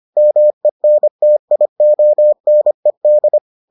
Morse code, heard as a keyed tone, about 25 wpm, 600 Hz. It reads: MENTIONED